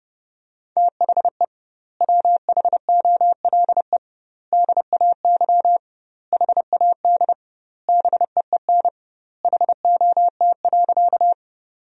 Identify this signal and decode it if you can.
Morse code, heard as a keyed tone, about 32 wpm, 700 Hz. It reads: THE WHOLE DAY HAD BEEN HOT.
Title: 2x Word-Spacing - Sentences from Top 500 Words - 30wpm